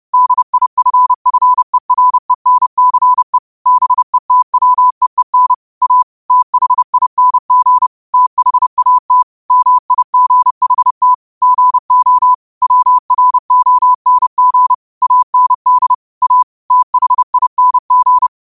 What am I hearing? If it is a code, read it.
Morse code, heard as a keyed tone, about 30 wpm, 1000 Hz. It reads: DIFFERENCE BETWEEN A THING THAT MIGHT GO WRONG AND A THING
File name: qotdfiles / 30wpm